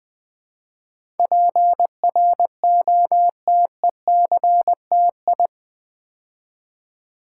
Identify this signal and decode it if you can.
Morse code, heard as a keyed tone, about 20 wpm, 700 Hz. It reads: PROTECTI
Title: Build A Word - 4 Letters At A Time - 10-Letter Words - 20wpm